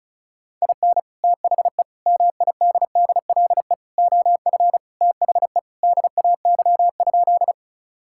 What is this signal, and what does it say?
Morse code, heard as a keyed tone, about 34 wpm, 700 Hz. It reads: IN THE MIDDLE OF THE DAY?
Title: Sentences from Top 1000 Words - Review Part 6 - 35wpm